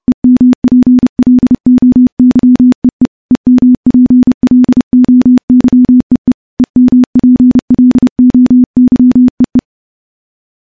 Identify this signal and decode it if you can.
Morse code, heard as a keyed tone, about 30 wpm, 250 Hz. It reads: EMPLOYEE EMPLOYEE EMPLOYEE
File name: morse / morse-training-30WPM